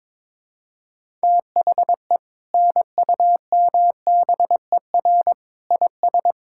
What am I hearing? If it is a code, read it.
Morse code, heard as a keyed tone, about 22 wpm, 700 Hz. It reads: THE NUMBER IS
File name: practice / 22wpm / Sentences with 2-Word Phrases - 22wpm